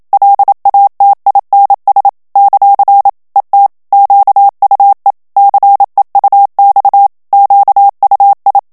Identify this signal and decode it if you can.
Morse code, heard as a keyed tone, about 27 wpm, 800 Hz. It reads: LATINS ; ET QUE CEUX QUI
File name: FRANCAIS TEXTE CLAIR 2 VITESSE 26 MOTS MINUTES